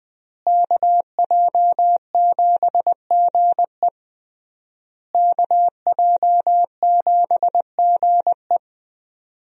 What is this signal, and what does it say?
Morse code, heard as a keyed tone, about 20 wpm, 700 Hz. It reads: KJ7GE KJ7GE
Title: USA Call Signs - 5 Characters - Repeated 3x - 20wpm